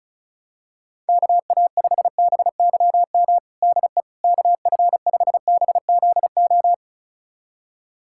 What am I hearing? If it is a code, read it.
Morse code, heard as a keyed tone, about 36 wpm, 700 Hz. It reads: KA5BYM DE KF5BZO
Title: USA Callsign DE Callsign 35wpm